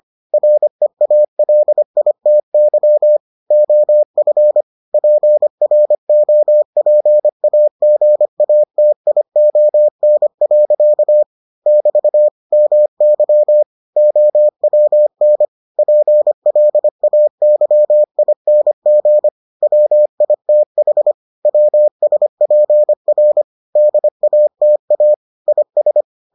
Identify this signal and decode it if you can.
Morse code, heard as a keyed tone, about 25 wpm, 600 Hz. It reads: REALITY OF PROPAGATION. <BT> MY OWN PLAYING WITH WSPR DATA IS